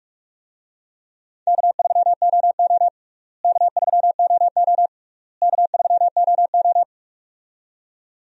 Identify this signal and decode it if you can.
Morse code, heard as a keyed tone, about 45 wpm, 700 Hz. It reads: K3OO K3OO K3OO
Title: USA Call Signs - 4 Characters - Repeated 3x - 45wpm